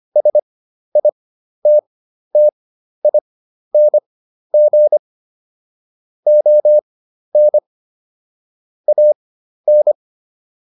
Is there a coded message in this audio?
Yes: Morse code from 0.2 to 9.9 s, heard as a keyed tone, about 25 wpm, spaced as for 12 wpm, 600 Hz. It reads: SITTING ON AN